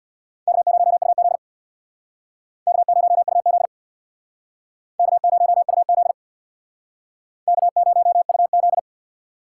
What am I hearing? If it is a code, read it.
Morse code, heard as a keyed tone, about 61 wpm, 700 Hz. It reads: K0UZ K0UZ K0UZ K0UZ
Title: Speed Racing - USA 1x2 Call Signs 50wpm